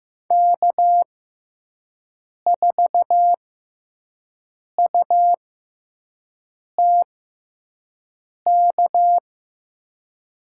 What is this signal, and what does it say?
Morse code, heard as a keyed tone, about 15 wpm, 700 Hz. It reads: K 4 U T K
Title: USA Call Signs - 5 Characters - 6x Character Spacing - 15wpm